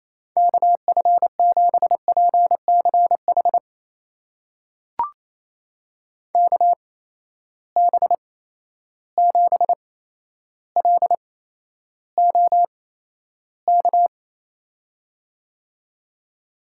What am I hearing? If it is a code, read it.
Morse code, heard as a keyed tone, about 27 wpm, spaced as for 12 wpm, 700 Hz. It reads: KF7PCH KB7LOK
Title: USA Call Signs - 6 Characters - 8x Character Spacing - 28wpm